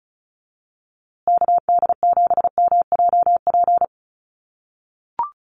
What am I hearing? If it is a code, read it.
Morse code, heard as a keyed tone, about 34 wpm, 700 Hz. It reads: KD7MJP
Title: USA Call Signs Repeated 3x 35wpm